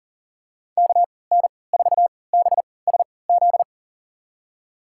Morse code, heard as a keyed tone, about 37 wpm, spaced as for 22 wpm, 700 Hz. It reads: KN4BSZ
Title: USA Call Signs - 6 Characters - 3x Character Spacing - 40wpm